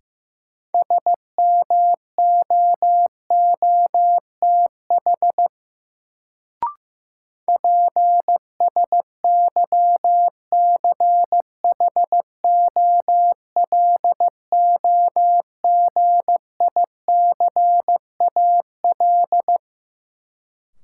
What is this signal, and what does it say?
Morse code, heard as a keyed tone, about 15 wpm, 700 Hz. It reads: SMOOTH PSYCHOLOGICAL
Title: Sentences from Top 1400 Words - New Word Drill Alt - 15wpm